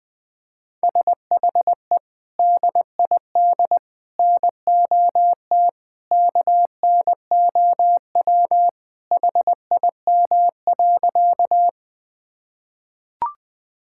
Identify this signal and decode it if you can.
Morse code, heard as a keyed tone, about 20 wpm, 700 Hz. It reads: SHE DID NOT KNOW HIM.
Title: Sentences with 3-Word Phrases - 20wpm